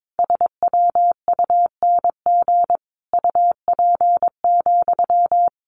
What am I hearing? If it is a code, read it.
Morse code, heard as a keyed tone, about 22 wpm, 700 Hz. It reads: SWUNG UP,
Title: A Sound Of Thunder 22wpm